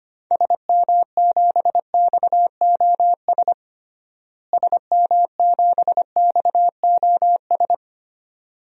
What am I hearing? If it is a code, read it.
Morse code, heard as a keyed tone, about 25 wpm, 700 Hz. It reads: SM7XOS SM7XOS
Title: DXCC-100 Call Signs - 6 Characters - Repeated 2x - 25wpm